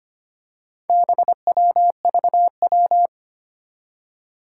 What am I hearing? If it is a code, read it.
Morse code, heard as a keyed tone, about 25 wpm, 700 Hz. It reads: BWVW
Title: Four Challenging Letter-Number Combinations 25wpm